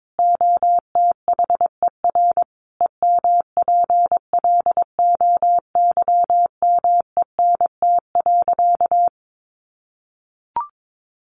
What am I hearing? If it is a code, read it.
Morse code, heard as a keyed tone, about 22 wpm, 700 Hz. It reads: OTHER EMPLOYMENT.
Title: Sentences from Top 1500 Words 22wpm